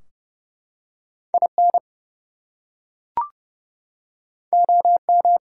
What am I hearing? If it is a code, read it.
Morse code, heard as a keyed tone, about 32 wpm, 700 Hz. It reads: IN OM
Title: Top 180 QSO Elements 30wpm